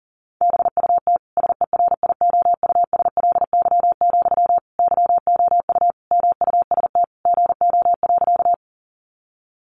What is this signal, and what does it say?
Morse code, heard as a keyed tone, about 40 wpm, 700 Hz. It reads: BUT SERIOUSLY, YOU MUST GO.
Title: Sentences from Top 1200 Words 40wpm